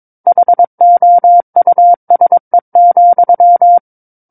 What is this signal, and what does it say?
Morse code, heard as a keyed tone, about 22 wpm, 700 Hz. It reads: HOUSE,
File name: 06-Chapter-06 22 WPM